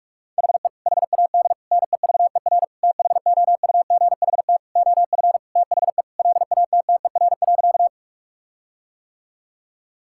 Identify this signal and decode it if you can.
Morse code, heard as a keyed tone, about 41 wpm, 700 Hz. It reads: HE HAD NEVER THOUGHT OF THE LATTER.
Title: Sentences from Top 1600 Words 45wpm